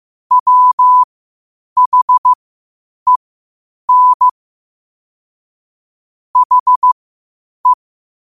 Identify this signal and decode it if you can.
Morse code, heard as a keyed tone, about 15 wpm, spaced as for 8 wpm, 1000 Hz. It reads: WHEN HE
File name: qotdfiles / 8wpm